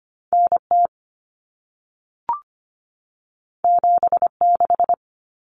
Morse code, heard as a keyed tone, about 25 wpm, 700 Hz. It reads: NT 76
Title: Common Cut Numbers - 2-Digit 25wpm